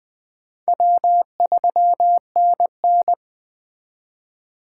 Morse code, heard as a keyed tone, about 23 wpm, 700 Hz. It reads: W3NN
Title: Speed Racing - USA 1x2 Call Signs 20wpm